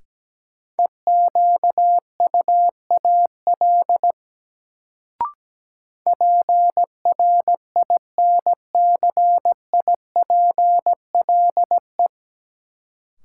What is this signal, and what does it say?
Morse code, heard as a keyed tone, about 17 wpm, 700 Hz. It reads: EQUAL PRINCIPLE
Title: Sentences from Top 1400 Words - New Word Drill Alt - 17wpm